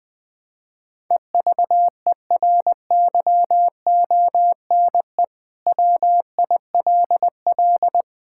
Morse code, heard as a keyed tone, about 20 wpm, 700 Hz. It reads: EVERYONE WILL
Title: Sentences from Top 1000 Words - Review Part 5 - 20wpm